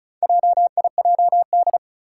Morse code, heard as a keyed tone, about 34 wpm, 700 Hz. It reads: JIJD
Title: Four Challenging Letter-Number Combinations 35wpm